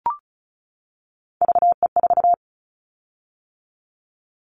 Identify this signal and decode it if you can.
Morse code, heard as a keyed tone, about 34 wpm, 700 Hz. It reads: VE4
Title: Callsign Prefixes for Top 50 DXCC - 35wpm